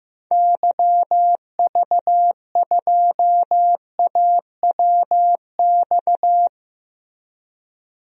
Morse code, heard as a keyed tone, about 15 wpm, 700 Hz. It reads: YV2AWX
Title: DXCC-100 Call Signs - 6 Characters - Repeated 2x - 15wpm